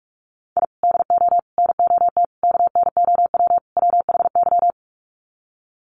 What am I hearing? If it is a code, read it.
Morse code, heard as a keyed tone, about 44 wpm, 700 Hz. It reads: I DO NOT KNOW WHY
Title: Sets of 5 Words 45wpm